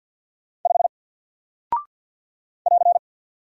Morse code, heard as a keyed tone, about 48 wpm, 700 Hz. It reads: 5 <AR>
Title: Single Character (Letter:Number:Punctuation:Prosign) Alt 50wpm